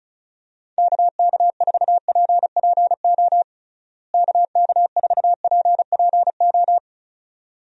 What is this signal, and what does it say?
Morse code, heard as a keyed tone, about 34 wpm, 700 Hz. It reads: KK4PPO KK4PPO
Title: USA Call Signs - 6 Characters - Repeated 2x - 35wpm